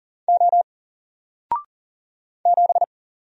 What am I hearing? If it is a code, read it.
Morse code, heard as a keyed tone, about 40 wpm, 700 Hz. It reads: O 7
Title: Single Letter-Number 40wpm